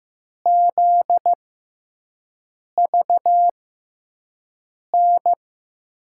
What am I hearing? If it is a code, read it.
Morse code, heard as a keyed tone, about 15 wpm, spaced as for 5 wpm, 700 Hz. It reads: ZVN
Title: USA Call Signs - 6 Characters - 6x Character Spacing - 15wpm